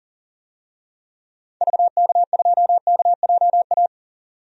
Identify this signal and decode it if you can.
Morse code, heard as a keyed tone, about 40 wpm, 700 Hz. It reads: VK2KJA
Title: DXCC-100 Call Signs - 6 Characters - 40wpm